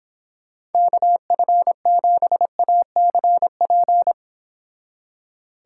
Morse code, heard as a keyed tone, about 29 wpm, 700 Hz. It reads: KF7ACP